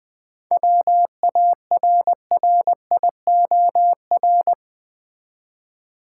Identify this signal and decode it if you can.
Morse code, heard as a keyed tone, about 20 wpm, 700 Hz. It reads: WARRIOR
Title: 7 Letter Words - First 3 Letters Given - 20wpm